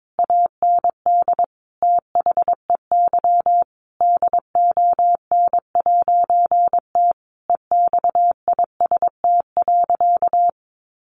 Morse code, heard as a keyed tone, about 22 wpm, 700 Hz. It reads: AND THEY DON'T EXIST.
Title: A Sound Of Thunder 22wpm